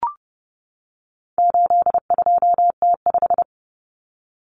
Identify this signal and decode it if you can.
Morse code, heard as a keyed tone, about 30 wpm, 700 Hz. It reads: 82T5
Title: Common Cut Numbers - 4-Digit 30wpm